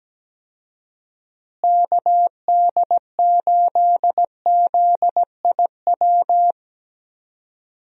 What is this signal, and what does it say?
Morse code, heard as a keyed tone, about 17 wpm, 700 Hz. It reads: KD8ZIW
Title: USA Call Signs - 6 Characters - Repeated 2x - 17wpm